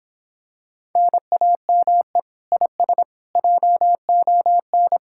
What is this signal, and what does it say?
Morse code, heard as a keyed tone, about 26 wpm, 700 Hz. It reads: NAME IS JON